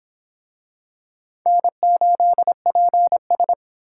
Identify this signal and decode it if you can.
Morse code, heard as a keyed tone, about 23 wpm, 700 Hz. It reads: N8PS